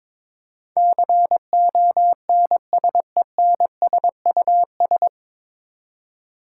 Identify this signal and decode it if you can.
Morse code, heard as a keyed tone, about 21 wpm, 700 Hz. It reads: CONSENSUS
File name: 9 Letter Words - First 3 Letters Given - 22wpm